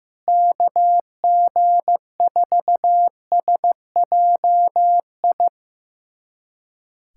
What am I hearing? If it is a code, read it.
Morse code, heard as a keyed tone, about 15 wpm, 700 Hz. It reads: KG4SJI
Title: USA Callsign DE Callsign 15wpm